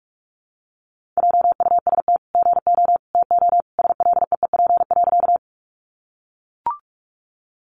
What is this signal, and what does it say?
Morse code, heard as a keyed tone, about 44 wpm, 700 Hz. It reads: JUST GO TO SLEEP.
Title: Sentences from Top 500 Words 45wpm